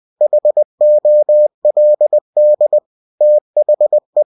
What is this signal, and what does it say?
Morse code, heard as a keyed tone, about 20 wpm, 600 Hz. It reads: HOLD THE
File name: cw-books / winnie-the-pooh / winnie-20wpm-page-0119